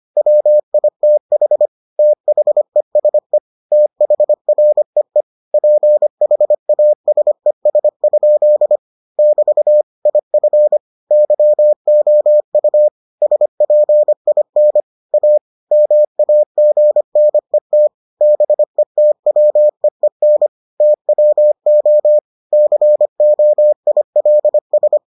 Morse code, heard as a keyed tone, about 25 wpm, 600 Hz. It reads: WITH THESE THREE PHASES? <BT> IF YOU SPIN A MAGNET BETWEEN TWO COILS